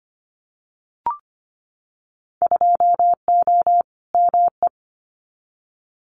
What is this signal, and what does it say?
Morse code, heard as a keyed tone, about 25 wpm, 700 Hz. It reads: 2O ME
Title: Winter Field Day Exchanges 25wpm